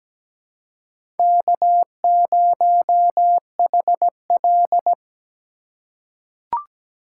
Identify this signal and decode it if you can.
Morse code, heard as a keyed tone, about 17 wpm, 700 Hz. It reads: K0HL
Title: USA Call Signs - 4 Characters - 4x Character Spacing - 17wpm